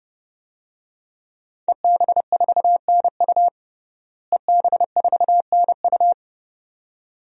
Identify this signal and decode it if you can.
Morse code, heard as a keyed tone, about 32 wpm, 700 Hz. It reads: EB4NU EB4NU
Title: DXCC-100 Call Signs - 5 Characters - Repeated 2x - 30wpm